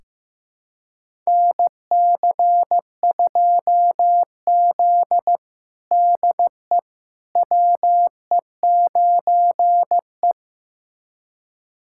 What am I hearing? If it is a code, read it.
Morse code, heard as a keyed tone, about 16 wpm, 700 Hz. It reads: NC2Z DE WE9E
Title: USA Callsign DE Callsign - 4 Characters - 15wpm